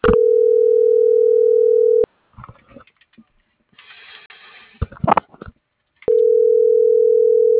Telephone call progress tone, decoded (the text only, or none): ringback tone